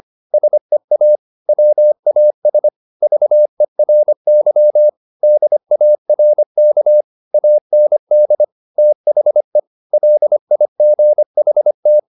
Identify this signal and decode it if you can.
Morse code, heard as a keyed tone, about 25 wpm, 600 Hz. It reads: SEA WAS VERY DARK AND THE LIGHT
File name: the old man and the sea part 1 25wpm